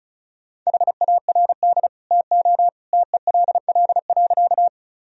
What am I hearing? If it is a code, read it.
Morse code, heard as a keyed tone, about 34 wpm, 700 Hz. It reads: HARD TO TELL.
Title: Sentences from Top 1000 Words - Review Part 1 - 35wpm